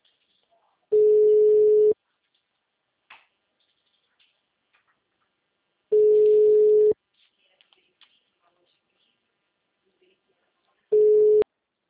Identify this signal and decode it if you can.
Ringback tone (Europe, 425 Hz).